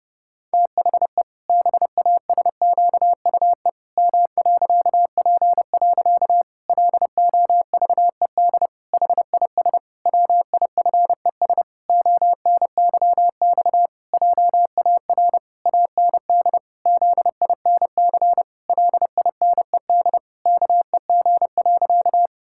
Morse code, heard as a keyed tone, about 30 wpm, 700 Hz. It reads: THE BASQUE M.P. LOVED HIS WIFES ONYX JAR AND ZINC LINED KEG.
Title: Pangram Challenge v2 - 30wpm